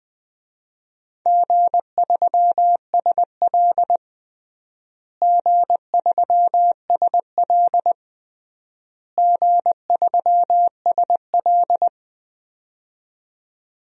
Morse code, heard as a keyed tone, about 21 wpm, 700 Hz. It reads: G3SL G3SL G3SL
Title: DXCC-100 Call Signs - 4 Characters - Repeated 3x - 20wpm